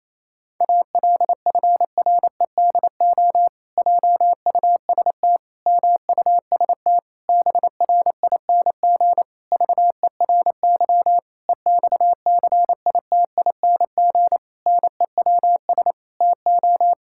Morse code, heard as a keyed tone, about 28 wpm, 700 Hz. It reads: ALFREDO JUST MUST BRING VERY EXCITING NEWS TO
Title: Pangram Challenge v2 - 28wpm